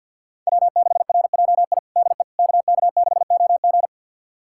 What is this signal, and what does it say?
Morse code, heard as a keyed tone, about 47 wpm, 700 Hz. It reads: W6RJI DE KK6OG
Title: USA Callsign DE Callsign - 5 Characters - 50wpm